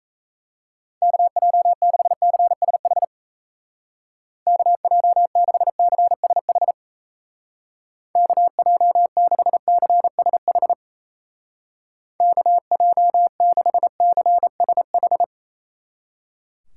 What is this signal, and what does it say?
Morse code, heard as a keyed tone, about 35 wpm, 700 Hz. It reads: KJ6CSH KJ6CSH KJ6CSH KJ6CSH
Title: Speed Racing - USA 2x3 Call Signs 28wpm